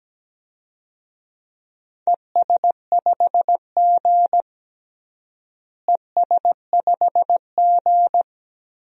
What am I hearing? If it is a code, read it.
Morse code, heard as a keyed tone, about 17 wpm, 700 Hz. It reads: ES5G ES5G
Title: DXCC-100 Call Signs - 4 Characters - Repeated 3x - 17wpm